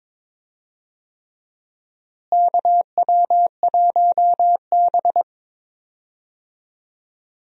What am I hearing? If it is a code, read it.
Morse code, heard as a keyed tone, about 22 wpm, 700 Hz. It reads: KW1B
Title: USA Call Signs - 4 Characters - Repeated 3x - 22wpm